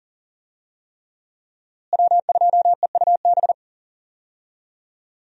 Morse code, heard as a keyed tone, about 40 wpm, 700 Hz. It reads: W2EUB
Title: USA Call Signs - 5 Characters - Repeated 3x - 40wpm